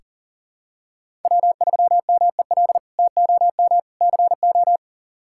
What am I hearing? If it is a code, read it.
Morse code, heard as a keyed tone, about 40 wpm, 700 Hz. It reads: W3MEL TOM CO
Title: K1USN SST QSO Practice - 40wpm